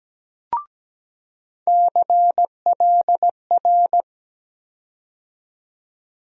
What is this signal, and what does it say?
Morse code, heard as a keyed tone, about 17 wpm, 700 Hz. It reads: CLR
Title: USA Call Sign Postfixes - 3 Letters 17wpm